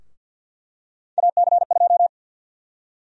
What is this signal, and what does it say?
Morse code, heard as a keyed tone, about 48 wpm, 700 Hz. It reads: AC2
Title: USA Call Sign Prefixes - Repeated 2x - 50wpm